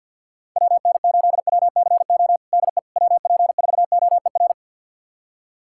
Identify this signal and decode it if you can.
Morse code, heard as a keyed tone, about 53 wpm, 700 Hz. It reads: WN8WCO DE WP4OER
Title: USA Callsign DE Callsign - 6 Characters - 50wpm